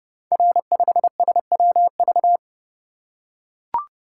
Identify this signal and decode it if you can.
Morse code, heard as a keyed tone, about 30 wpm, 700 Hz. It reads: R5SWV
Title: Code Groups - 5 Letter:Numbers - 30wpm